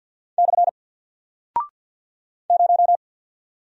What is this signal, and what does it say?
Morse code, heard as a keyed tone, about 49 wpm, 700 Hz. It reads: / 0
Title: Single Character (Letter:Number:Punctuation:Prosign) Alt 50wpm